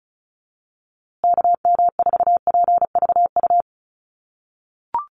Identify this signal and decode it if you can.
Morse code, heard as a keyed tone, about 34 wpm, 700 Hz. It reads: KM4PVU
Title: USA Call Signs Repeated 3x 35wpm